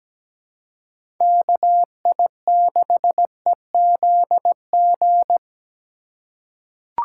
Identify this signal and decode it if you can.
Morse code, heard as a keyed tone, about 17 wpm, 700 Hz. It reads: KI6EZG
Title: USA Call Signs - 6 Characters - 2x Character Spacing - 17wpm